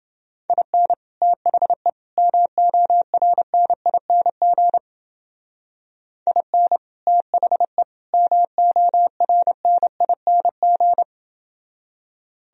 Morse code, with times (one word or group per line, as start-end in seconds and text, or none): 0.5-0.9 IN
1.2-1.9 THE
2.2-4.8 MORNING
6.3-6.8 IN
7.1-7.8 THE
8.1-11.0 MORNING